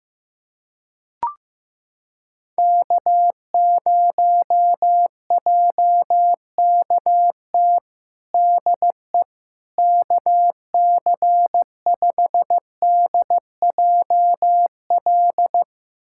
Morse code, heard as a keyed tone, about 15 wpm, 700 Hz. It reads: K0JKT DE KC5DJL
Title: USA Callsign DE Callsign 15wpm